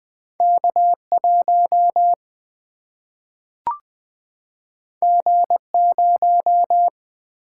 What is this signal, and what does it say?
Morse code, heard as a keyed tone, about 20 wpm, 700 Hz. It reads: K1 G0
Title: Single-Letter Single-Number 20wpm